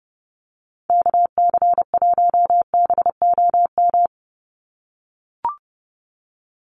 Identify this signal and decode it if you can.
Morse code, heard as a keyed tone, about 30 wpm, 700 Hz. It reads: KC1BOM
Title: USA Call Signs 30wpm